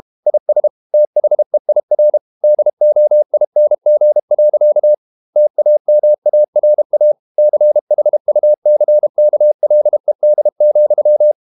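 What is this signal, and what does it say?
Morse code, heard as a keyed tone, about 32 wpm, 600 Hz. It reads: IS THEIR DOING. TAMARA CHUCKLED,
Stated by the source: Morse code practice at 32 wpm from CW med Gnister, reading Ken Follett's novel Never in English.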